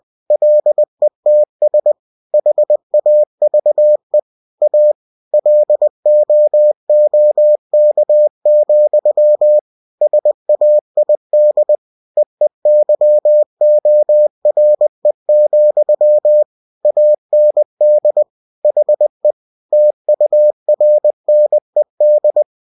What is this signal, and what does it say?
Morse code, heard as a keyed tone, about 20 wpm, 600 Hz. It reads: LETS HAVE A LOOK, SAID EEYORE, AND HE TURNED
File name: cw-books / winnie-the-pooh / winnie-20wpm-page-0027